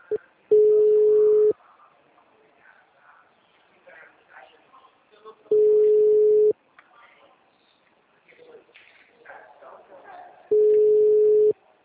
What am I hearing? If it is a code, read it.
Ringback tone (Europe, 425 Hz), with speech.